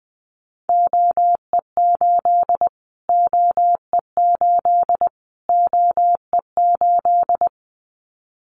Morse code, heard as a keyed tone, about 20 wpm, 700 Hz. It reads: OE8 OE8 OE8
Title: Callsign Prefixes for Top 25 DXCC - Repeated 3x - 20wpm